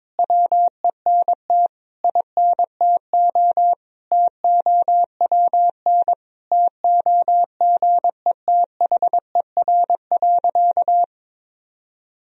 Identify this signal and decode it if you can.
Morse code, heard as a keyed tone, about 22 wpm, 700 Hz. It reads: WENT INTO TOWN TOGETHER.
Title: Sentences from Top 1000 Words - Review Part 2 - 22wpm